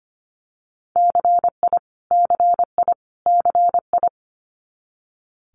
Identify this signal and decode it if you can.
Morse code, heard as a keyed tone, about 25 wpm, 700 Hz. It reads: CI CI CI
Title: Multi-letter Phonemes Repeated 3x 25wpm